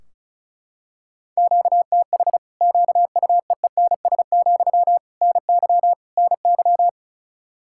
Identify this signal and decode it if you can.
Morse code, heard as a keyed tone, about 36 wpm, 700 Hz. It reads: QTH QUEENS, NY NY
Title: Practice QSO - Exchange 1 - Beginning - 35wpm